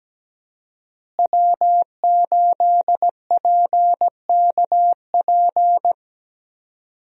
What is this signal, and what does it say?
Morse code, heard as a keyed tone, about 17 wpm, 700 Hz. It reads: W8PKP
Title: USA Call Signs - 5 Characters - Repeated 2x - 17wpm